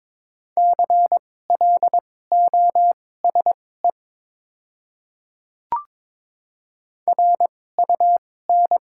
Morse code, heard as a keyed tone, about 22 wpm, spaced as for 16 wpm, 700 Hz. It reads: CLOSE RUN
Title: 2x Character Spacing - Top 300 Words - 22wpm